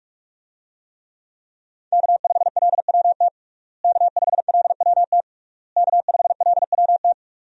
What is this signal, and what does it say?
Morse code, heard as a keyed tone, about 46 wpm, 700 Hz. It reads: K5LWT K5LWT K5LWT
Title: USA Call Signs - 5 Characters - Repeated 3x - 45wpm